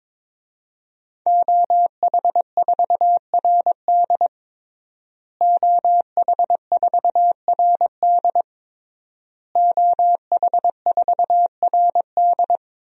Morse code, heard as a keyed tone, about 22 wpm, 700 Hz. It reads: OH4RD OH4RD OH4RD
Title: DXCC-100 Call Signs - 5 Characters - Repeated 3x - 22wpm